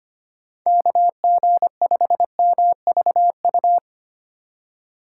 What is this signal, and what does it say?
Morse code, heard as a keyed tone, about 25 wpm, 700 Hz. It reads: KG5MVU
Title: USA Call Signs - 6 Characters - 2x Character Spacing - 25wpm